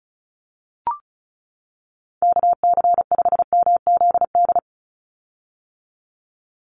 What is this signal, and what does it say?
Morse code, heard as a keyed tone, about 34 wpm, 700 Hz. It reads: KC5MZD
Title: USA Call Signs 35wpm